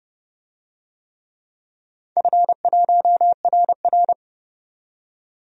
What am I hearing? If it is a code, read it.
Morse code, heard as a keyed tone, about 30 wpm, 700 Hz. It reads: F1RR
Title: DXCC-100 Call Signs - 4 Characters - 30wpm